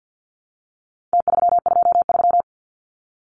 Morse code, heard as a keyed tone, about 50 wpm, 700 Hz. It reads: T323
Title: Numbers - Contest Style - 4-Digit 50wpm